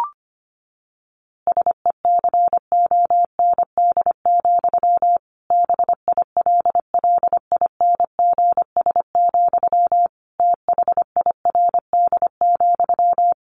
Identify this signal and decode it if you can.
Morse code, heard as a keyed tone, about 25 wpm, 700 Hz. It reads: SECOND, BILLINGS, THIRD,
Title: A Sound Of Thunder 25wpm